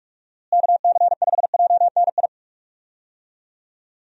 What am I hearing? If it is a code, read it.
Morse code, heard as a keyed tone, about 43 wpm, 700 Hz. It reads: KC5JNI
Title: USA Call Signs - 6 Characters - Repeated 3x - 45wpm